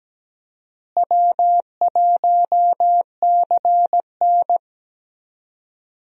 Morse code, heard as a keyed tone, about 17 wpm, 700 Hz. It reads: W1CN
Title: USA Call Signs - 4 Characters - Repeated 3x - 17wpm